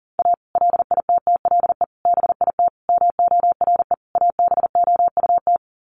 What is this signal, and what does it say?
Morse code, heard as a keyed tone, about 40 wpm, 700 Hz. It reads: A LITTLE BIT MORE ABOUT
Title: Sets of 5 Words Encore2 40wpm